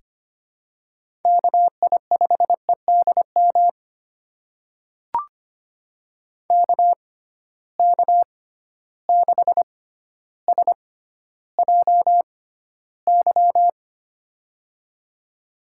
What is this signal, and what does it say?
Morse code, heard as a keyed tone, about 25 wpm, spaced as for 13 wpm, 700 Hz. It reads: KI5EDM KK6SJY
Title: USA Call Signs - 6 Characters - 6x Character Spacing - 25wpm